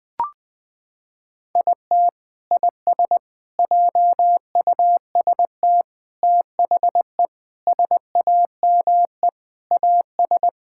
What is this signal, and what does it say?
Morse code, heard as a keyed tone, about 20 wpm, 700 Hz. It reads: IT IS JUST THE SAME AS
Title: Sentences from Top 1000 Words - Review Part 6 - 20wpm